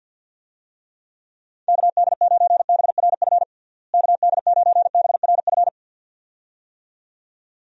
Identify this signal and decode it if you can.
Morse code, heard as a keyed tone, about 47 wpm, 700 Hz. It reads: KD9BRF KD9BRF
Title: USA Call Signs - 6 Characters - Repeated 2x - 50wpm